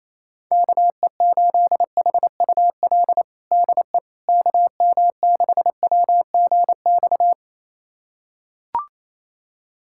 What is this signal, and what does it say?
Morse code, heard as a keyed tone, about 28 wpm, 700 Hz. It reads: KE8HUL DE KM6WGX
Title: USA Callsign DE Callsign 28wpm